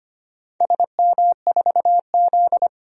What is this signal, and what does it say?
Morse code, heard as a keyed tone, about 25 wpm, 690 Hz. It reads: SM4Z
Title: DXCC-100 Call Signs - 4 Characters - 25wpm